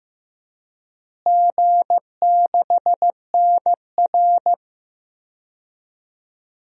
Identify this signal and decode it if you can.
Morse code, heard as a keyed tone, about 15 wpm, 700 Hz. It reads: G6NR
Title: DXCC-100 Call Signs - 4 Characters - Repeated 3x - 15wpm